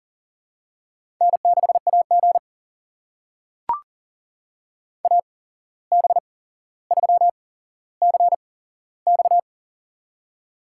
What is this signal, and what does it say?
Morse code, heard as a keyed tone, about 40 wpm, spaced as for 16 wpm, 700 Hz. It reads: N6AG AB3CX
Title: USA Call Signs - 8x Character Spacing - 40wpm